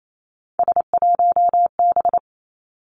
Morse code, heard as a keyed tone, about 28 wpm, 700 Hz. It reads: S1B
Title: Three Challenging Letter-Number Combinations 28wpm